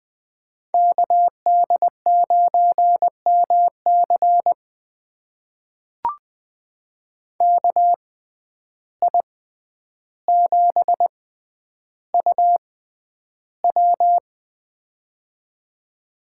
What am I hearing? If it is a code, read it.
Morse code, heard as a keyed tone, about 20 wpm, spaced as for 10 wpm, 700 Hz. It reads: KD9MC KI7UW
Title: USA Call Signs - 5 Characters - 6x Character Spacing - 20wpm